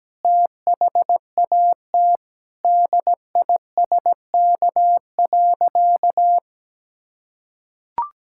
Morse code, heard as a keyed tone, about 17 wpm, 700 Hz. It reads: THAT DISK.
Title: Sentences from Top 1000 Words - Review Part 6 - 17wpm